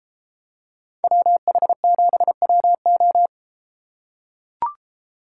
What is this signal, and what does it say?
Morse code, heard as a keyed tone, about 31 wpm, 700 Hz. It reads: WH7WO